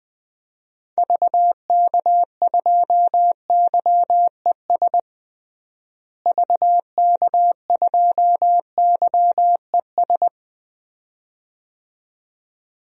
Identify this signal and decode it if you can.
Morse code, heard as a keyed tone, about 20 wpm, 700 Hz. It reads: VK2YES VK2YES
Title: DXCC-100 Call Signs - 6 Characters - Repeated 2x - 20wpm